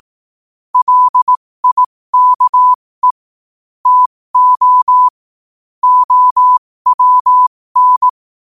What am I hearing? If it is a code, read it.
Morse code, heard as a keyed tone, about 18 wpm, spaced as for 15 wpm, 1000 Hz. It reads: LIKE TO OWN
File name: qotdfiles / 15wpm